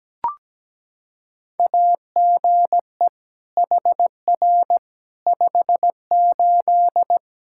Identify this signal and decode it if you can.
Morse code, heard as a keyed tone, about 17 wpm, 700 Hz. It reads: AGE HR 58
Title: Practice QSO - Exchange 3 - More Info - 17wpm